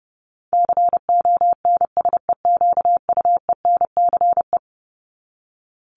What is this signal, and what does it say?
Morse code, heard as a keyed tone, about 30 wpm, 700 Hz. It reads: CONSEQUENCE
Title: LY Suffix - Top 100 Words with Base Words 30wpm